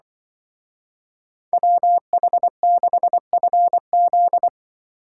Morse code, heard as a keyed tone, about 23 wpm, 700 Hz. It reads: WH6FZ
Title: Speed Racing - USA 2x2 Call Signs 20wpm